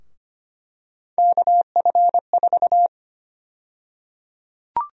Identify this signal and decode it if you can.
Morse code, heard as a keyed tone, about 25 wpm, 700 Hz. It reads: KF4
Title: USA Call Sign Prefixes - 25wpm